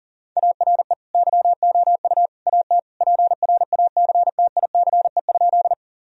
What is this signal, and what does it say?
Morse code, heard as a keyed tone, about 40 wpm, 700 Hz. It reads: ARE YOU AT PRACTICE?
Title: Sentences from Top 1000 Words - Review Part 4 - 40wpm